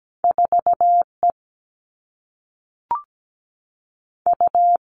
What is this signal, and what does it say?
Morse code, heard as a keyed tone, about 17 wpm, 700 Hz. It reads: 4 E U
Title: Two Challenging Letter-Number Combinations 17wpm